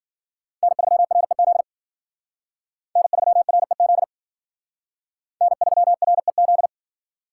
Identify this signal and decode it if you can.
Morse code, heard as a keyed tone, about 51 wpm, 700 Hz. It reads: N3REZ N3REZ N3REZ